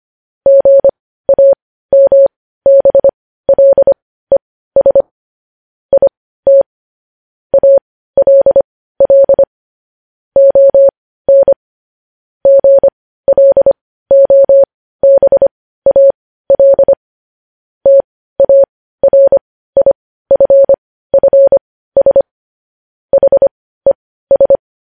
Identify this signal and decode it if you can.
Morse code, heard as a keyed tone, about 25 wpm, spaced as for 15 wpm, 550 Hz. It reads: GAMBLES IT ALL ON GLOBAL TARIFFS HES
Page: news headlines in morse code at 15 wpm